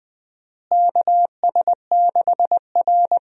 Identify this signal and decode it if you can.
Morse code, heard as a keyed tone, about 21 wpm, 700 Hz. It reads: KS6R
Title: USA Call Signs - 4 Characters - Repeated 2x - 20wpm